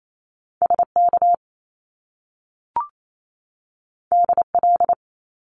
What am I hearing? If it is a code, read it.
Morse code, heard as a keyed tone, about 28 wpm, 700 Hz. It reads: SK DL
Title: Top Two Letter Combinations in Words 28wpm